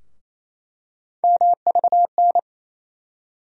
Morse code, heard as a keyed tone, about 27 wpm, 700 Hz. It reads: MVN
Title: USA Call Sign Postfixes - 3 Letters 28wpm